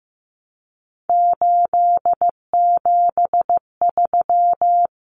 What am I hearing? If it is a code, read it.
Morse code, heard as a keyed tone, about 15 wpm, 700 Hz. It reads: 873
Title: Numbers - 3-Digit 15wpm